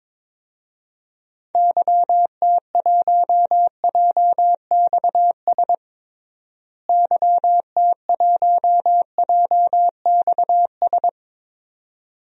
Morse code, heard as a keyed tone, about 22 wpm, 700 Hz. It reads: YT1JXS YT1JXS
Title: DXCC-100 Call Signs - 6 Characters - Repeated 2x - 22wpm